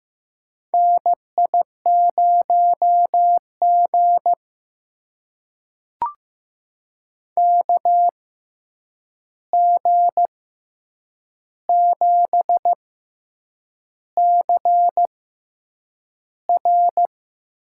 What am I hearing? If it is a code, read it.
Morse code, heard as a keyed tone, about 15 wpm, spaced as for 7 wpm, 700 Hz. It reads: NI0G KG7CR
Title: USA Call Signs - 6x Character Spacing - 15wpm